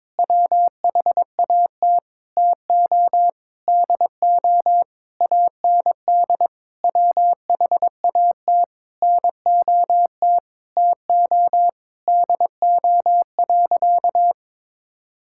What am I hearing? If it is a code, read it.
Morse code, heard as a keyed tone, about 22 wpm, 700 Hz. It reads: WHAT TO DO AND WHAT NOT TO DO.
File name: practice / 22wpm / Sentences with 8-Word Phrases - 22wpm